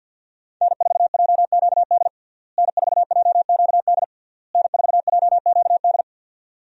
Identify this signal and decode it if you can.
Morse code, heard as a keyed tone, about 47 wpm, 700 Hz. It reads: N4JQD N4JQD N4JQD
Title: USA Call Signs - 5 Characters - Repeated 3x - 50wpm